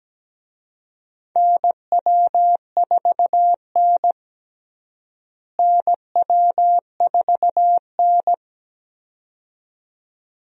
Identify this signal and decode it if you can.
Morse code, heard as a keyed tone, about 17 wpm, 700 Hz. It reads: NW4N NW4N
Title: USA Call Signs - 4 Characters - Repeated 2x - 17wpm